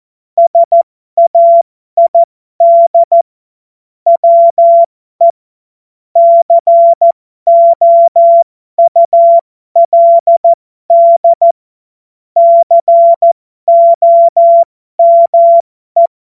Morse code, heard as a keyed tone, about 14 wpm, spaced as for 12 wpm, 680 Hz. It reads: SAID WE COULD COME